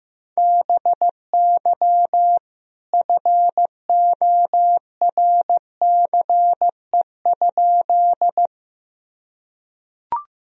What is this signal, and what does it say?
Morse code, heard as a keyed tone, about 16 wpm, 700 Hz. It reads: BY FORCE?
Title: Sentences from Top 1000 Words - Review Part 3 - 15wpm